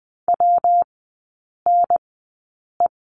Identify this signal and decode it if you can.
Morse code, heard as a keyed tone, about 20 wpm, spaced as for 8 wpm, 700 Hz. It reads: WNE
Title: 166 Lesson 27e - Number 7 - Three Characters 20wpm